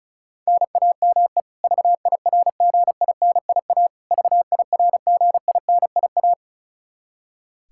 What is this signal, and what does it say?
Morse code, heard as a keyed tone, about 34 wpm, 700 Hz. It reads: NAME VIRGINIA VIRGINIA
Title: Practice QSO - Exchange 1 - Beginning - 35wpm